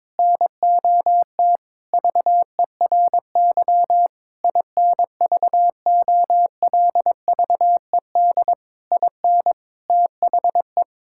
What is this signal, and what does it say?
Morse code, heard as a keyed tone, about 21 wpm, 700 Hz. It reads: NOT VERY INVOLVED IN THE
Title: Sentences from Top 1000 Words - Review Part 5 - 22wpm